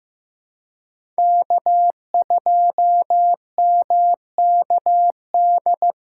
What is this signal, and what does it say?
Morse code, heard as a keyed tone, about 16 wpm, 700 Hz. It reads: K2MKD
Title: USA Callsign DE Callsign - 5 Characters - 15wpm